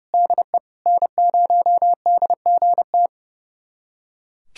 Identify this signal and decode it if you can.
Morse code, heard as a keyed tone, about 30 wpm, 700 Hz. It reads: DE N0DGT
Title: USA Callsign DE Callsign - 5 Characters - 30wpm